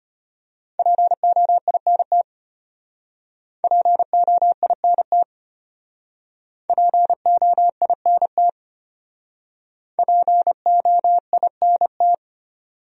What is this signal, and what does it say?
Morse code, heard as a keyed tone, about 32 wpm, 700 Hz. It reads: POINT POINT POINT POINT